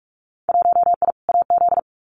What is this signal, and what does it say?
Morse code, heard as a keyed tone, about 46 wpm, 700 Hz. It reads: 1I AZ
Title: Winter Field Day Exchanges 45wpm